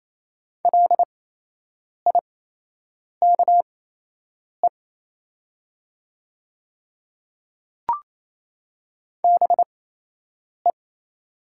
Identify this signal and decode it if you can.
Morse code, heard as a keyed tone, about 27 wpm, spaced as for 8 wpm, 700 Hz. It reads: LIKE BE
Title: 8x Character Spacing - Top 100 Words - 28wpm